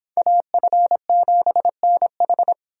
Morse code, heard as a keyed tone, about 29 wpm, 700 Hz. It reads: AF7NH